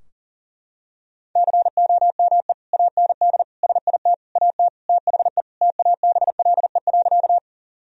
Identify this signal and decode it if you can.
Morse code, heard as a keyed tone, about 40 wpm, 700 Hz. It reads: COME AND SIT AT THE TABLE.
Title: Sentences from Top 1000 Words - Review Part 2 - 40wpm